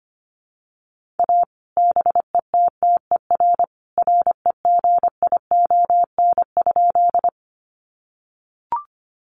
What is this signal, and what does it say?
Morse code, heard as a keyed tone, about 25 wpm, 700 Hz. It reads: A BETTER REGION?
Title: Sentences from Top 1500 Words 25wpm